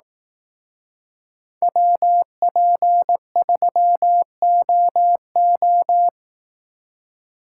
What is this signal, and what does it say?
Morse code, heard as a keyed tone, about 18 wpm, 700 Hz. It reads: WP3OO
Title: Speed Racing - USA 2x2 Call Signs 15wpm